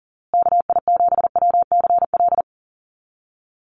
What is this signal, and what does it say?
Morse code, heard as a keyed tone, about 40 wpm, 700 Hz. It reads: KI7WCL
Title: USA Call Signs Repeated 3x 40wpm